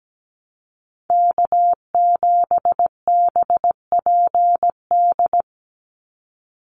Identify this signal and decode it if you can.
Morse code, heard as a keyed tone, about 17 wpm, 700 Hz. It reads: K7BPD
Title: USA Call Signs 17wpm